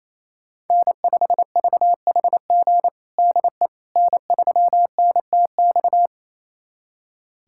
Morse code, heard as a keyed tone, about 28 wpm, 700 Hz. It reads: N5VHG DE N3NTX
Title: USA Callsign DE Callsign - 5 Characters - 28wpm